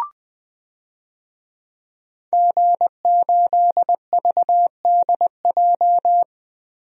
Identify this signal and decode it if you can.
Morse code, heard as a keyed tone, about 20 wpm, 700 Hz. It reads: G8VDJ
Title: DXCC-100 Call Signs - 5 Characters - Repeated 2x - 20wpm